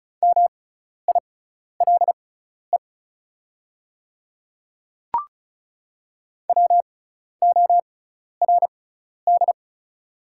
Morse code, heard as a keyed tone, about 36 wpm, spaced as for 12 wpm, 700 Hz. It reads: MILE WORD
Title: 6x Character Spacing - Top 500 Words - 35wpm